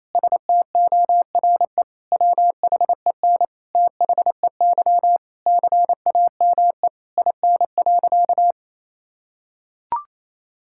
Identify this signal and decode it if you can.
Morse code, heard as a keyed tone, about 28 wpm, 700 Hz. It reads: STORE WHEN THEY CAME IN.
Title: Sentences from Top 1000 Words - Review Part 5 - 28wpm